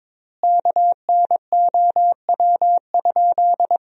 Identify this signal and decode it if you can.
Morse code, heard as a keyed tone, about 22 wpm, 700 Hz. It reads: KNOW?
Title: 3x Word-Spacing - Sentences from Top 200 Words - 22wpm